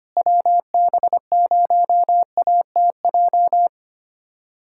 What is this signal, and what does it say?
Morse code, heard as a keyed tone, about 25 wpm, 700 Hz. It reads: WB0ATJ
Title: USA Callsign DE Callsign 25wpm